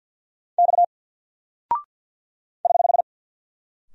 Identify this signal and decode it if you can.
Morse code, heard as a keyed tone, about 48 wpm, 700 Hz. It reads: X <HH>
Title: Single Character (Letter:Number:Punctuation:Prosign) Alt 50wpm